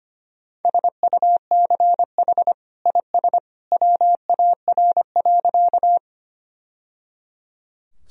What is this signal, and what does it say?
Morse code, heard as a keyed tone, about 25 wpm, 700 Hz. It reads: SUCH IS WAR.
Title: Sentences from Top 1000 Words - Review Part 2 - 25wpm